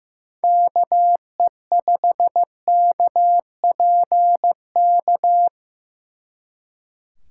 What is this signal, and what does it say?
Morse code, heard as a keyed tone, about 17 wpm, 700 Hz. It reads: KE5KPK
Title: Speed Racing - USA Call Signs 15wpm